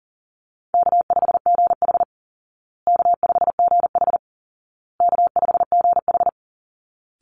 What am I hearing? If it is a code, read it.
Morse code, heard as a keyed tone, about 40 wpm, 700 Hz. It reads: K5GH K5GH K5GH
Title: Call Signs Repeated 2x - 40wpm